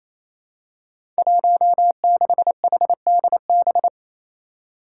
Morse code, heard as a keyed tone, about 27 wpm, 700 Hz. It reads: 16HDB
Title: Five Challenging Letter-Number Combinations 28wpm